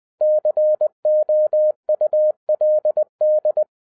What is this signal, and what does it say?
Morse code, heard as a keyed tone, about 20 wpm, 610 Hz. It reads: COULD